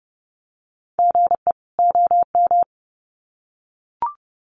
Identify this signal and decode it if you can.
Morse code, heard as a keyed tone, about 30 wpm, 700 Hz. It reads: GE OM
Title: Common QSO Phrases 30wpm